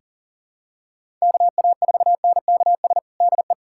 Morse code, heard as a keyed tone, about 40 wpm, 700 Hz. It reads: KA4NKS DE
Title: USA Callsign DE Callsign 40wpm